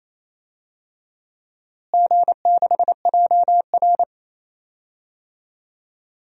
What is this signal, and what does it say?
Morse code, heard as a keyed tone, about 28 wpm, 700 Hz. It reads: G6JR
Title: DXCC-100 Call Signs - 4 Characters - Repeated 3x - 28wpm